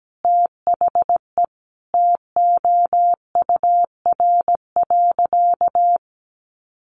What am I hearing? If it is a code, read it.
Morse code, heard as a keyed tone, about 17 wpm, 700 Hz. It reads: THE TOUR.
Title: Sentences from Top 1400 Words 17wpm